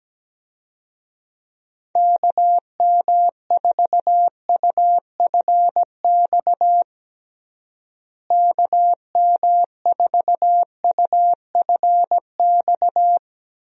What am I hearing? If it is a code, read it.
Morse code, heard as a keyed tone, about 17 wpm, 700 Hz. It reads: KM4UFX KM4UFX
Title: USA Call Signs - 6 Characters - Repeated 2x - 17wpm